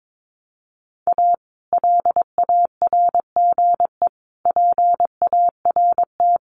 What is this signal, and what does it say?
Morse code, heard as a keyed tone, about 22 wpm, 700 Hz. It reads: A LARGE PART
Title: Sets of 5 Words Encore2 22wpm